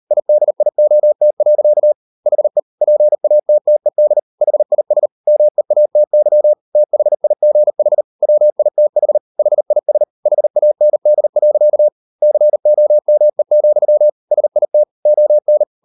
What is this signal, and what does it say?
Morse code, heard as a keyed tone, about 38 wpm, 600 Hz. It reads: IDIOT. HE PATTED HIS MEATY THIGH WITH HIS HAND. COME, SIT ON